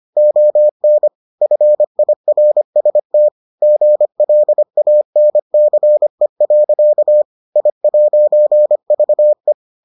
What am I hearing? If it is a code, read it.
Morse code, heard as a keyed tone, about 25 wpm, 600 Hz. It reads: ON FIRST GLANCE. I'VE